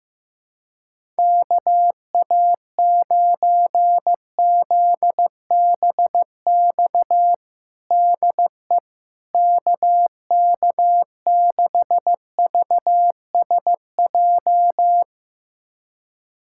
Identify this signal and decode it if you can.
Morse code, heard as a keyed tone, about 15 wpm, 700 Hz. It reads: KA9ZBX DE KK6VSJ
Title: USA Callsign DE Callsign - 6 Characters - 15wpm